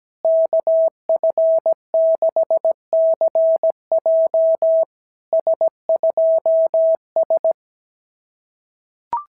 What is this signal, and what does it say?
Morse code, heard as a keyed tone, about 17 wpm, 650 Hz. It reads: KF6CJ S2S
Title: SOTA QSO Practice 17wpm